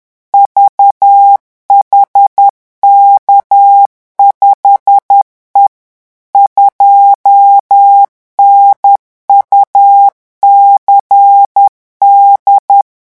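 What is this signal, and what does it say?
Morse code, heard as a keyed tone, about 10 wpm, 800 Hz. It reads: VHK5E 2NUCD
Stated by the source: Morse revision exercise for French radio amateurs at 10 words per minute.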